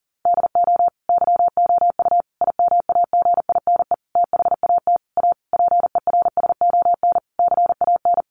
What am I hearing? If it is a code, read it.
Morse code, heard as a keyed tone, about 40 wpm, 700 Hz. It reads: DO YOU IMAGINE THAT A PERSON CAN
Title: Sentences from Top 1400 Words 40wpm